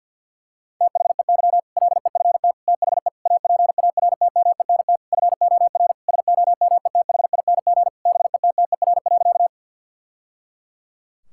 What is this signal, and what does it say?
Morse code, heard as a keyed tone, about 48 wpm, 700 Hz. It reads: THEY LEFT THE APARTMENT FOR SOMETHING BETTER.
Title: Sentences from Top 1000 Words - Review Part 6 - 50wpm